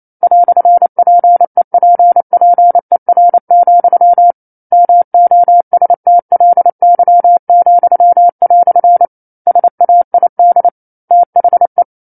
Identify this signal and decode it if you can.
Morse code, heard as a keyed tone, about 28 wpm, 700 Hz. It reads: "PEPPER, MOSTLY," SAID THE